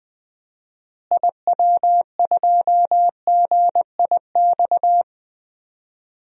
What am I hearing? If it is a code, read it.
Morse code, heard as a keyed tone, about 20 wpm, 700 Hz. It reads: IW2GIX
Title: DXCC-100 Call Signs - 6 Characters - Repeated 2x - 20wpm